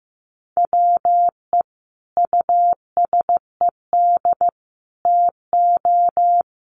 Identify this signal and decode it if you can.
Morse code, heard as a keyed tone, about 15 wpm, 700 Hz. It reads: WE USED TO
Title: Sentences from Top 1100 Words 15wpm